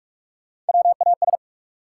Morse code, heard as a keyed tone, about 46 wpm, 700 Hz. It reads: WAS
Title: Sentences from Top 100 Words - New Word Drill - 45wpm